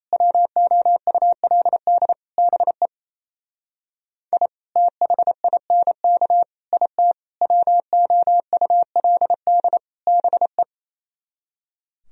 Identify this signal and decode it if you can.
Morse code, heard as a keyed tone, about 35 wpm, 700 Hz. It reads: WOULD BE I THINK IT WOULD BE